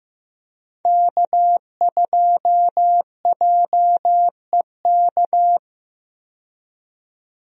Morse code, heard as a keyed tone, about 15 wpm, 700 Hz. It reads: K2JEK
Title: USA Call Signs - 5 Characters - 15wpm